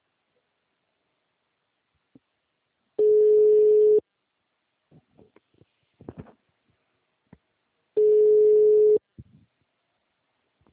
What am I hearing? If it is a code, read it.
Ringback tone (Europe, 425 Hz).